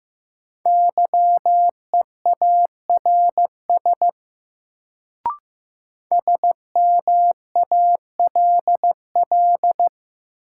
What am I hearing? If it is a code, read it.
Morse code, heard as a keyed tone, about 15 wpm, 700 Hz. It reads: YEARS SMALL
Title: Sentences from Top 200 Words - New Word Drill - 15wpm